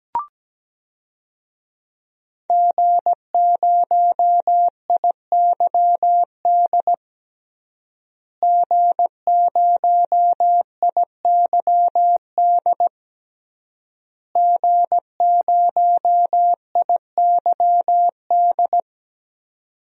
Morse code, heard as a keyed tone, about 17 wpm, 700 Hz. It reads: G0IYD G0IYD G0IYD
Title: DXCC-100 Call Signs - 5 Characters - Repeated 3x - 17wpm